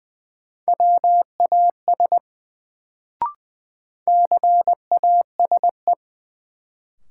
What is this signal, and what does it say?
Morse code, heard as a keyed tone, about 20 wpm, 700 Hz. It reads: WAS CASE
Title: Sets of 2 Words - New Word Drill - 20wpm